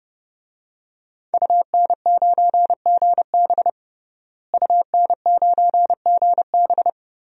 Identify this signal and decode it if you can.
Morse code, heard as a keyed tone, about 32 wpm, 700 Hz. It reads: UN9GB UN9GB
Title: DXCC-100 Call Signs - 5 Characters - Repeated 2x - 30wpm